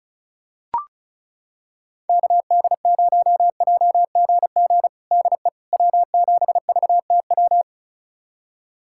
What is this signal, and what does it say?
Morse code, heard as a keyed tone, about 36 wpm, 700 Hz. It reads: KD0JGG DE W7VTW
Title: USA Callsign DE Callsign 35wpm